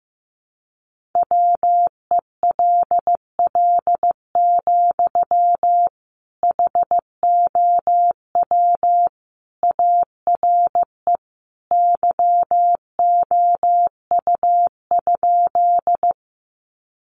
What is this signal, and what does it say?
Morse code, heard as a keyed tone, about 15 wpm, 700 Hz. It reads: WELL, HOW ARE YOU?
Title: Sentences from Top 200 Words 15wpm